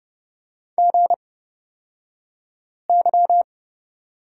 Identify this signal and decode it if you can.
Morse code, heard as a keyed tone, about 30 wpm, spaced as for 5 wpm, 700 Hz. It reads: GY
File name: Lesson 37d - Letter x - Two Characters 30wpm